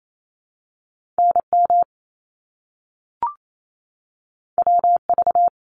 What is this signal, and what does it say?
Morse code, heard as a keyed tone, about 28 wpm, 700 Hz. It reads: NM WV
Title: ARRL Sections 28wpm